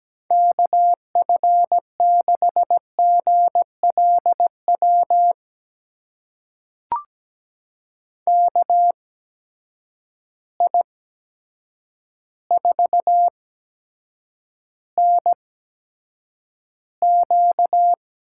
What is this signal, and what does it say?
Morse code, heard as a keyed tone, about 17 wpm, spaced as for 8 wpm, 700 Hz. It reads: KF6GLW KI4NQ
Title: USA Call Signs - 6 Characters - 8x Character Spacing - 17wpm